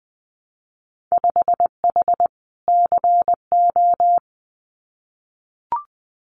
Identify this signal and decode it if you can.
Morse code, heard as a keyed tone, about 20 wpm, 700 Hz. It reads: 5H CO
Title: Winter Field Day Exchanges Encore 20wpm